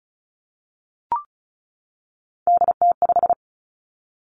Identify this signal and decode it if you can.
Morse code, heard as a keyed tone, about 36 wpm, 700 Hz. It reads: DT5
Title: Three Challenging Letter-Number Combinations 35wpm